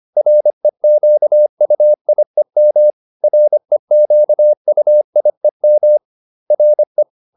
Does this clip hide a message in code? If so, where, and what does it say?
Yes: Morse code from 0.2 to 7.0 s, heard as a keyed tone, about 25 wpm, 600 Hz. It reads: REQUIEM REQUIEM RE